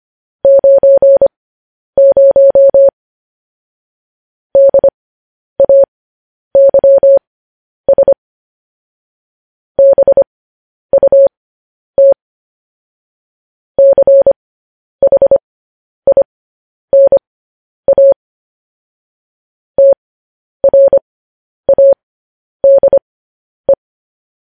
Morse code, heard as a keyed tone, about 25 wpm, spaced as for 10 wpm, 550 Hz. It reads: 90 DAYS BUT CHINA TRADE
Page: news headlines in morse code at 10 wpm